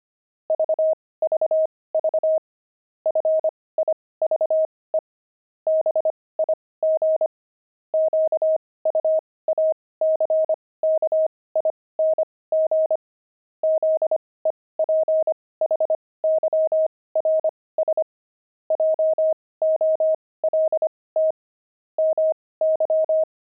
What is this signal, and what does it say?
Morse code, heard as a keyed tone, about 25 wpm, spaced as for 18 wpm, 630 Hz. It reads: VVV FIVE BIG QUACKING ZEPHYRS JOLT MY
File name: pangram-fs-18-cs-25